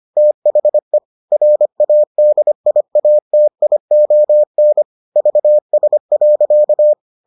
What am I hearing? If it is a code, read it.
Morse code, heard as a keyed tone, about 25 wpm, 600 Hz. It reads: THE RADIATION VS.